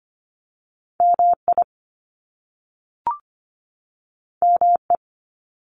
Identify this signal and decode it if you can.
Morse code, heard as a keyed tone, about 25 wpm, 700 Hz. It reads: MI ME
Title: US States 25wpm